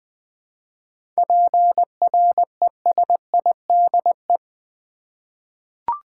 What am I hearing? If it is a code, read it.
Morse code, heard as a keyed tone, about 20 wpm, 700 Hz. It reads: PRESIDE
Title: 7 Letter Words - First 3 Letters Given - 20wpm